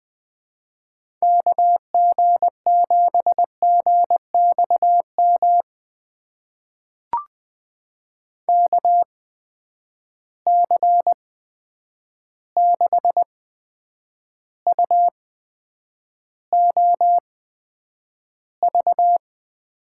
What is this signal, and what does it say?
Morse code, heard as a keyed tone, about 20 wpm, spaced as for 9 wpm, 700 Hz. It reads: KG7GXM KC6UOV
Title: USA Call Signs - 6 Characters - 8x Character Spacing - 20wpm